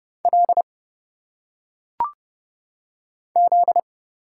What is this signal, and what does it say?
Morse code, heard as a keyed tone, about 29 wpm, 700 Hz. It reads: L Z
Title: Single Letters - Phonetic Pronunciation 30wpm